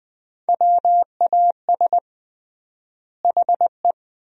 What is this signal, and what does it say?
Morse code, heard as a keyed tone, about 20 wpm, 700 Hz. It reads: WAS HE
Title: 3x Word-Spacing - Sentences from Top 400 Words - 20wpm